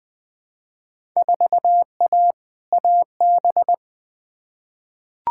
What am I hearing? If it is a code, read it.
Morse code, heard as a keyed tone, about 20 wpm, 700 Hz. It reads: 4A AB
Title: Field Day Exchanges Encore - 20wpm